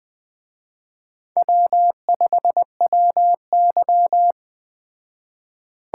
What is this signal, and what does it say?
Morse code, heard as a keyed tone, about 20 wpm, 700 Hz. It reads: W5WY
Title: Speed Racing - USA 1x2 Call Signs 17wpm